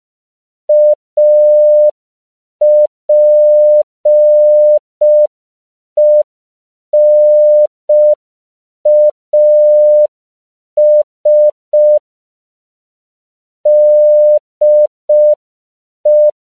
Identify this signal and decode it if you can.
Morse code, heard as a keyed tone, about 5 wpm, 600 Hz. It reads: APENAS DE